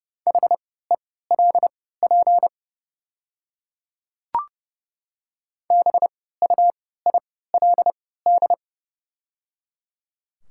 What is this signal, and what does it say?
Morse code, heard as a keyed tone, about 30 wpm, spaced as for 17 wpm, 700 Hz. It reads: HELP BUILD
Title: 3x Character Spacing - Top 200 Words - 30wpm